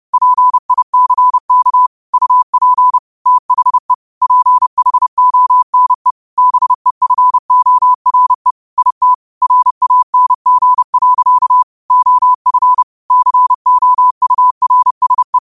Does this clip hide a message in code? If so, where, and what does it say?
Yes: Morse code from 0.1 to 15.4 s, heard as a keyed tone, about 30 wpm, 1000 Hz. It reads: PICK UP THE PHONE BEFORE IT RANG. OF COURSE